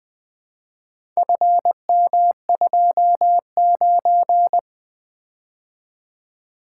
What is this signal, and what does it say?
Morse code, heard as a keyed tone, about 20 wpm, 700 Hz. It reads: FM29
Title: Four Letter-Number Combinations - 20wpm